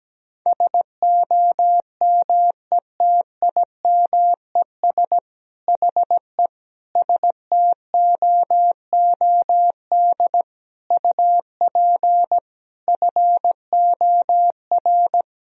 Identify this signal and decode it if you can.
Morse code, heard as a keyed tone, about 17 wpm, 700 Hz. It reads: SOMETIMES HE STOOD UP FOR
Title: Sentences from Top 1000 Words - Review Part 4 - 17wpm